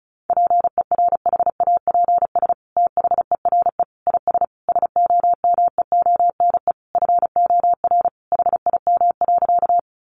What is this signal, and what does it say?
Morse code, heard as a keyed tone, about 34 wpm, 700 Hz. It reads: PERHAPS THERE IS SOMEONE FOR HIM.
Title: Sentences from Top 600 Words 35wpm